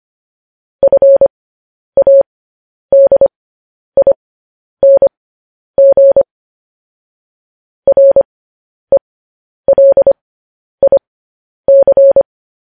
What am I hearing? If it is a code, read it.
Morse code, heard as a keyed tone, about 25 wpm, spaced as for 10 wpm, 550 Hz. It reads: FADING RELIC
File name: podcast / News headlines-1x-25x10